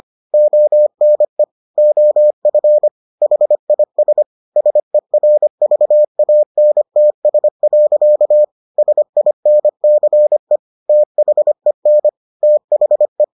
Morse code, heard as a keyed tone, about 25 wpm, 600 Hz. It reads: ONE OF HIS SERVANTS. SINCE THEN THE